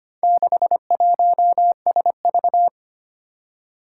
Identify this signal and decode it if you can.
Morse code, heard as a keyed tone, about 25 wpm, 700 Hz. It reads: 61SV
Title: Four Challenging Letter-Number Combinations 25wpm